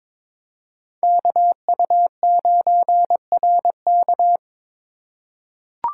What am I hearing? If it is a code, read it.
Morse code, heard as a keyed tone, about 22 wpm, 700 Hz. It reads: KU9RK
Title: K1USN SST QSO Practice - 22wpm